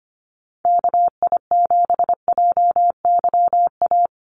Morse code, heard as a keyed tone, about 25 wpm, 700 Hz. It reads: KI7JYA
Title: USA Call Signs Repeated 3x 25wpm